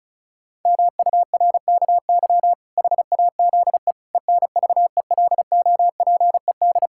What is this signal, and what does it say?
Morse code, heard as a keyed tone, about 34 wpm, 700 Hz. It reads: MURKY HAZE ENVELOPED
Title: Pangram Challenge v2 - 35wpm